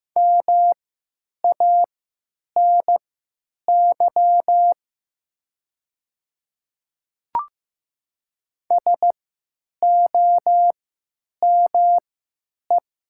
Morse code, heard as a keyed tone, about 15 wpm, spaced as for 9 wpm, 700 Hz. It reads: MANY SOME
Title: 3x Character Spacing - Top 300 Words - 15wpm